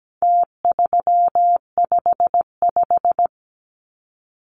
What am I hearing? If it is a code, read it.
Morse code, heard as a keyed tone, about 17 wpm, 700 Hz. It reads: T355
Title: Numbers - Contest Style - 4-Digit 17wpm